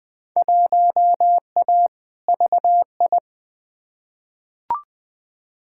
Morse code, heard as a keyed tone, about 20 wpm, 700 Hz. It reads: 1A VI
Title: Field Day Exchanges Encore - 20wpm